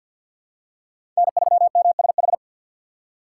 Morse code, heard as a keyed tone, about 50 wpm, 700 Hz. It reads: N3MSH
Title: USA Call Signs - 5 Characters - 2x Character Spacing - 50wpm